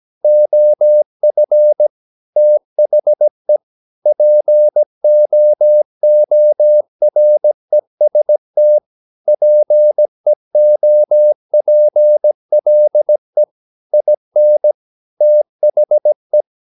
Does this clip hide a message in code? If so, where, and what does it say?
Yes: Morse code from 0.2 to 16.4 s, heard as a keyed tone, about 17 wpm, 600 Hz. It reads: OF THE POOREST PEOPLE IN THE